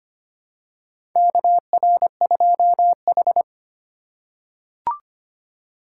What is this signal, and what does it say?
Morse code, heard as a keyed tone, about 25 wpm, 700 Hz. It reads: KR2H
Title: USA Call Signs - 4 Characters - 4x Character Spacing - 25wpm